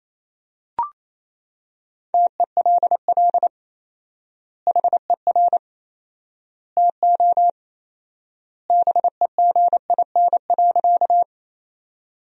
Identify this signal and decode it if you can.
Morse code, heard as a keyed tone, about 28 wpm, 700 Hz. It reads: TELL HER TO BEGIN.
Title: 4x Word-Spacing - Sentences from Top 300 Words - 28wpm